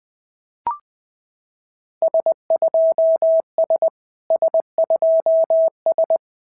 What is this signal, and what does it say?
Morse code, heard as a keyed tone, about 20 wpm, 690 Hz. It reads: S2S S2S
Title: SOTA QSO Practice 20wpm